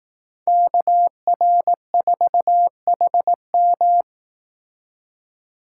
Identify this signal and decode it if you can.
Morse code, heard as a keyed tone, about 18 wpm, 700 Hz. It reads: KR4HM
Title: Speed Racing - USA 2x2 Call Signs 15wpm